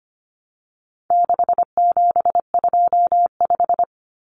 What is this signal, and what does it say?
Morse code, heard as a keyed tone, about 25 wpm, 700 Hz. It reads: 6725
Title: Numbers - 4-Digit 25wpm